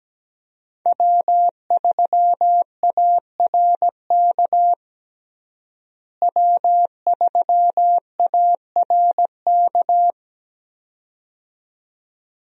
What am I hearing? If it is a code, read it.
Morse code, heard as a keyed tone, about 17 wpm, 700 Hz. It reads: W3ARK W3ARK
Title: USA Call Signs - 5 Characters - Repeated 2x - 17wpm